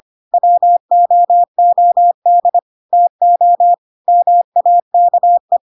Morse code, heard as a keyed tone, about 25 wpm, 700 Hz. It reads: WOOD TO MAKE
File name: Pinocchio / Pinocchio ch2 25 25 ik1hge